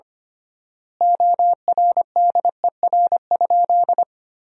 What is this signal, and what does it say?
Morse code, heard as a keyed tone, about 25 wpm, 700 Hz. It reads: ORDER?
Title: 3x Word-Spacing - Sentences from Top 400 Words - 25wpm